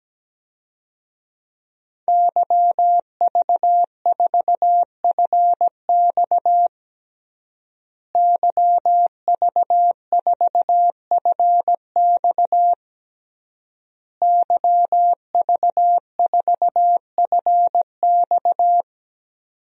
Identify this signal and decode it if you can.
Morse code, heard as a keyed tone, about 17 wpm, 700 Hz. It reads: YV4FX YV4FX YV4FX
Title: DXCC-100 Call Signs - 5 Characters - Repeated 3x - 17wpm